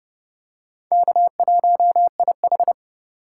Morse code, heard as a keyed tone, about 29 wpm, 700 Hz. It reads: K1IH